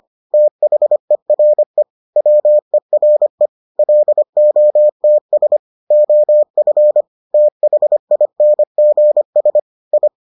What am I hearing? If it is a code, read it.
Morse code, heard as a keyed tone, about 25 wpm, 600 Hz. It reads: THERE WERE LOTS OF THINGS I